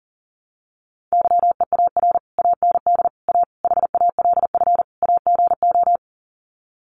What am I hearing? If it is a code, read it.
Morse code, heard as a keyed tone, about 40 wpm, 700 Hz. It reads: YEAR AND A HALF AGO
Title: Sets of 5 Words Encore 40wpm